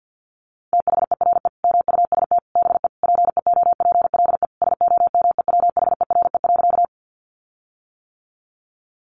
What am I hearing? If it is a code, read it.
Morse code, heard as a keyed tone, about 48 wpm, 700 Hz. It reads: THERE MUST BE PEOPLE SOMEWHERE.
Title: Sentences from Top 1400 Words 50wpm